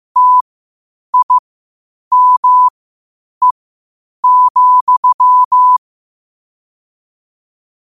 Morse code, heard as a keyed tone, about 15 wpm, spaced as for 8 wpm, 1000 Hz. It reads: TIME,
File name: qotdfiles / 8wpm